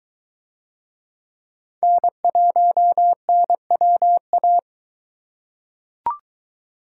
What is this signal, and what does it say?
Morse code, heard as a keyed tone, about 22 wpm, 700 Hz. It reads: N1NWA